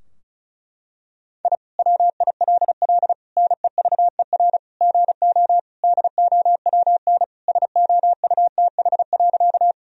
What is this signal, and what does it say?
Morse code, heard as a keyed tone, about 34 wpm, 700 Hz. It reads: I WILL NEVER GO DOWN SOUTH.
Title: Sentences from Top 1000 Words - Review Part 2 - 35wpm